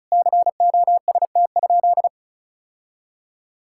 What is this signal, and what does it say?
Morse code, heard as a keyed tone, about 36 wpm, 700 Hz. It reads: COST?
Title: Sentences from Top 1000 Words - Review Part 6 - 35wpm